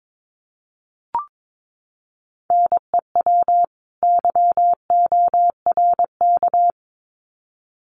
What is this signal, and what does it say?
Morse code, heard as a keyed tone, about 22 wpm, 700 Hz. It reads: NEW YORK
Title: Sets of 2 Words 22wpm